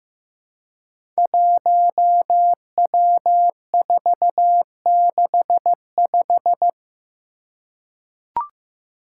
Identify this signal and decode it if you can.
Morse code, heard as a keyed tone, about 15 wpm, 700 Hz. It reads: 1W465
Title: Five Challenging Letter-Number Combinations 15wpm